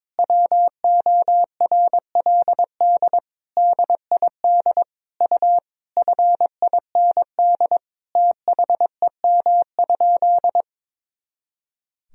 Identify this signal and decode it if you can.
Morse code, heard as a keyed tone, about 22 wpm, 700 Hz. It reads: WORLD DID U FIND THEM?
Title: Sentences from Top 200 Words and Abbreviations 22wpm